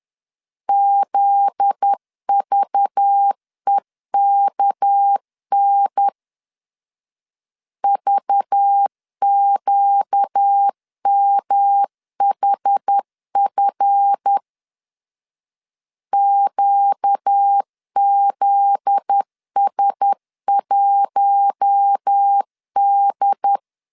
Morse code, heard as a keyed tone, about 10 wpm, 800 Hz. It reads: ZVEKN VQMHF QZS1D
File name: LEZIONI / 7 GF9Z / 7 GF9Z 10wpm 2611